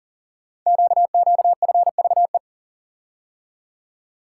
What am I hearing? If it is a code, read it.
Morse code, heard as a keyed tone, about 41 wpm, 690 Hz. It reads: QQFVE
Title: Code Groups - 5 Letter:Numbers - 40wpm